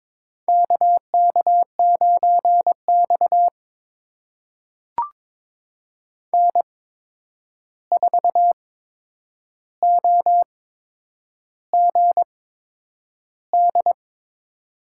Morse code, heard as a keyed tone, about 22 wpm, spaced as for 9 wpm, 700 Hz. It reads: KK9X N4OGD
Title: USA Call Signs - 8x Character Spacing - 22wpm